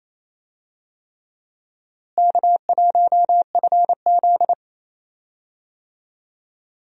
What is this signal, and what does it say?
Morse code, heard as a keyed tone, about 28 wpm, 700 Hz. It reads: K1FZ
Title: USA Call Signs - 4 Characters - Repeated 3x - 28wpm